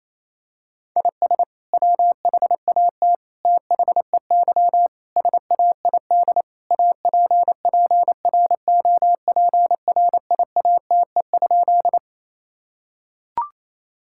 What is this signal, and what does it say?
Morse code, heard as a keyed tone, about 28 wpm, 700 Hz. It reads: IS WHAT THEY SAID APPROPRIATE?
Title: Sentences from Top 1000 Words - Review Part 5 - 28wpm